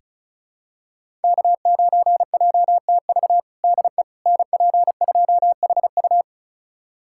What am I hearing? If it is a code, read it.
Morse code, heard as a keyed tone, about 34 wpm, 700 Hz. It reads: K9JTV DE NP2HU
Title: USA Callsign DE Callsign - 5 Characters - 35wpm